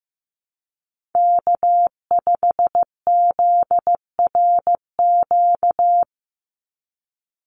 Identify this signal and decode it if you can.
Morse code, heard as a keyed tone, about 15 wpm, 700 Hz. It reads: K5ZRQ
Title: USA Call Signs 15wpm